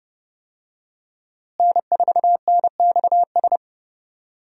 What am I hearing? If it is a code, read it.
Morse code, heard as a keyed tone, about 30 wpm, 700 Hz. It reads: N4NXS
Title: USA Call Signs - 5 Characters - Repeated 2x - 30wpm